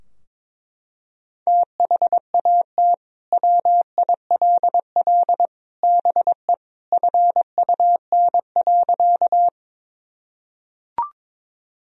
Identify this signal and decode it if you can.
Morse code, heard as a keyed tone, about 22 wpm, 700 Hz. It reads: THAT WILL BE FUN.
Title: Sentences from Top 1000 Words - Review Part 4 - 22wpm